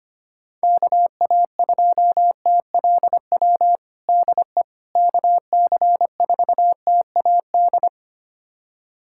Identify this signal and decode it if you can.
Morse code, heard as a keyed tone, about 25 wpm, 700 Hz. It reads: KA2TLW DE KC4TAD
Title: USA Callsign DE Callsign - 6 Characters - 25wpm